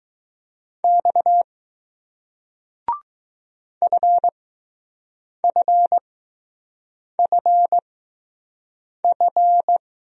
Morse code, heard as a keyed tone, about 20 wpm, spaced as for 6 wpm, 700 Hz. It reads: X FFFF